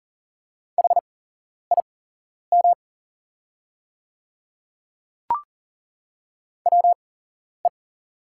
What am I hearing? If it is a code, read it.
Morse code, heard as a keyed tone, about 40 wpm, spaced as for 11 wpm, 700 Hz. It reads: HIM WE
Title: 8x Character Spacing - Top 300 Words - 40wpm